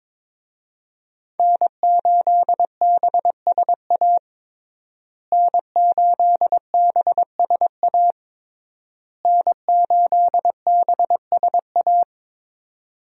Morse code, heard as a keyed tone, about 22 wpm, 700 Hz. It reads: N8BSA N8BSA N8BSA
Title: USA Call Signs - 5 Characters - Repeated 3x - 22wpm